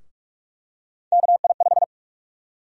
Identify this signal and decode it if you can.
Morse code, heard as a keyed tone, about 46 wpm, 700 Hz. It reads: KI5
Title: USA Call Sign Prefixes - 45wpm